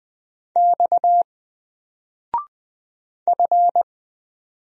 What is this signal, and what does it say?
Morse code, heard as a keyed tone, about 20 wpm, 700 Hz. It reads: X F
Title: Single Letter-Number 20wpm